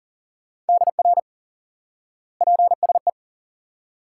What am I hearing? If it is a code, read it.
Morse code, heard as a keyed tone, about 40 wpm, 700 Hz. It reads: DR PSE
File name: IQE Course 3 - Lesson 89 - YAGI - Two Elements - 40wpm